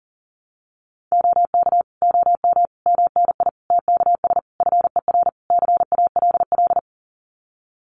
Morse code, heard as a keyed tone, about 40 wpm, 700 Hz. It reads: OK OM MNI TKS FER CALL
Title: Common QSO Phrases 40wpm